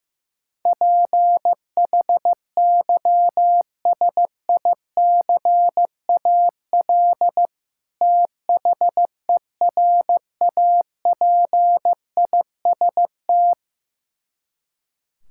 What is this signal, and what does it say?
Morse code, heard as a keyed tone, about 15 wpm, 700 Hz. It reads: PHYSICAL THERAPIST
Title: Practice QSO - Exchange 3 - More Info - 15wpm